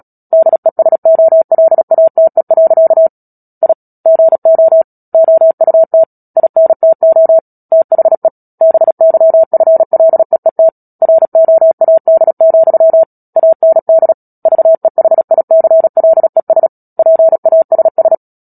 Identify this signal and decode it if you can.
Morse code, heard as a keyed tone, about 37 wpm, 650 Hz. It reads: DESOLATE. I GO OUT INTO THE BYFLEET ROAD, AND VEHICLES PASS
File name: Book Two / 10-Chapter 10 - Epilogue - 36 WPM